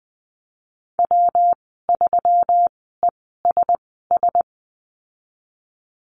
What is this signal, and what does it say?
Morse code, heard as a keyed tone, about 20 wpm, spaced as for 15 wpm, 700 Hz. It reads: W3ESS
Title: 161 Lesson 26g - Letter g - Call Signs 20wpm